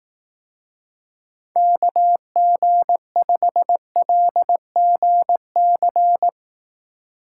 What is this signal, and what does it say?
Morse code, heard as a keyed tone, about 18 wpm, 700 Hz. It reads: KG5LGC
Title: Speed Racing - USA 2x3 Call Signs 15wpm